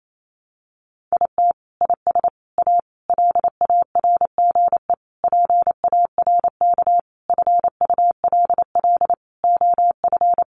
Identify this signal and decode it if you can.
Morse code, heard as a keyed tone, about 28 wpm, 700 Hz. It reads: IT IS A LARGE PARK FULL OF
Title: Sentences from Top 1300 Words 28wpm